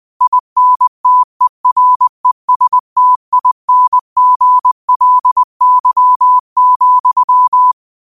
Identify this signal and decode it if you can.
Morse code, heard as a keyed tone, about 20 wpm, 1000 Hz. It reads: INTERESTINGLY,
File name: qotdfiles / 20wpm